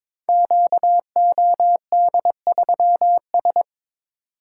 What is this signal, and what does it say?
Morse code, heard as a keyed tone, about 23 wpm, 700 Hz. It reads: QOD3S
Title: Code Groups - 5 Letter:Numbers - 22wpm